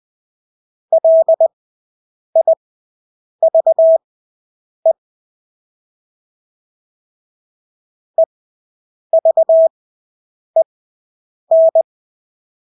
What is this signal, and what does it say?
Morse code, heard as a keyed tone, about 20 wpm, spaced as for 8 wpm, 650 Hz. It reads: LIVE EVEN
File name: Engelske ord 20/8 wpm